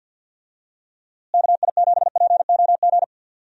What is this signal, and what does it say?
Morse code, heard as a keyed tone, about 50 wpm, 700 Hz. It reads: KI7POG
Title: USA Call Signs - 6 Characters - Repeated 3x - 50wpm